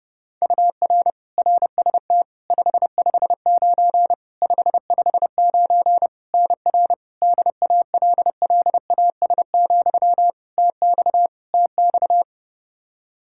Morse code, heard as a keyed tone, about 32 wpm, 700 Hz. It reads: UR RST 559 559 NR DALLAS, TX TX
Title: Practice QSO - Exchange 1 - Beginning - 30wpm